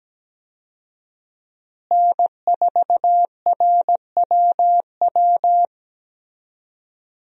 Morse code, heard as a keyed tone, about 17 wpm, 700 Hz. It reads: N4RWW
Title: USA Call Signs - 5 Characters - Repeated 2x - 17wpm